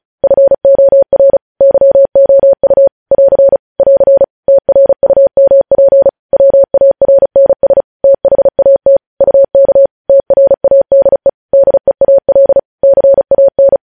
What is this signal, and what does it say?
Morse code, heard as a keyed tone, about 36 wpm, 550 Hz. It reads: FOR YOU <AR> <AR> TRUMP WARNS THAT UK TRADE DEAL CAN